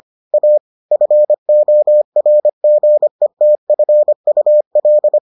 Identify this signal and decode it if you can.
Morse code, heard as a keyed tone, about 25 wpm, 600 Hz. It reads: A FORGETFUL